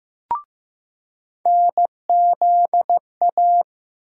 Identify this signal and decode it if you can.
Morse code, heard as a keyed tone, about 16 wpm, 700 Hz. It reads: NZA
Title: USA Call Sign Postfixes - 3 Letters 15wpm